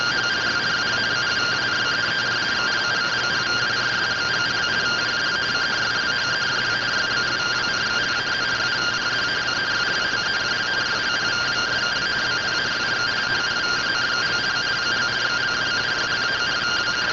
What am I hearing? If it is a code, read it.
Two-tone teleprinter signal, about 170 Hz shift.